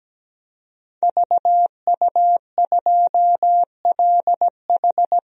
Morse code, heard as a keyed tone, about 17 wpm, 700 Hz. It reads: VU2LH
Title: DXCC-100 Call Signs - 5 Characters - 17wpm